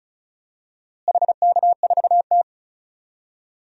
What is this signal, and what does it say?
Morse code, heard as a keyed tone, about 36 wpm, 700 Hz. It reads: HK4T
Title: DXCC-100 Call Signs - 4 Characters - Repeated 2x - 35wpm